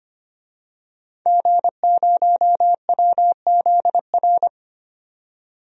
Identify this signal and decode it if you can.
Morse code, heard as a keyed tone, about 25 wpm, 700 Hz. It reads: G0WZR
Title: DXCC-100 Call Signs - 5 Characters - 25wpm